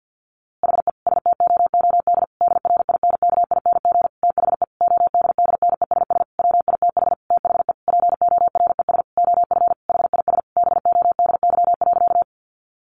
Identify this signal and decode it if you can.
Morse code, heard as a keyed tone, about 50 wpm, 700 Hz. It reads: HE STOOD DRINKING THE ODDNESS WITH THE PORES OF HIS BODY.
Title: A Sound Of Thunder 50wpm